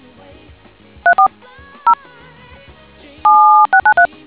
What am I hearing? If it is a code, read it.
DTMF, heard as telephone touch tones, tones of about 100 ms. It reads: 34 0 7693